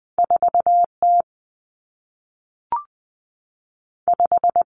Morse code, heard as a keyed tone, about 20 wpm, 700 Hz. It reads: 4 T 5
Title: Two Challenging Letter-Number Combinations 20wpm